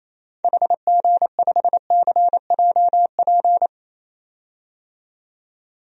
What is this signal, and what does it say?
Morse code, heard as a keyed tone, about 27 wpm, 700 Hz. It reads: HG5CJP
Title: DXCC-100 Call Signs - 6 Characters - 28wpm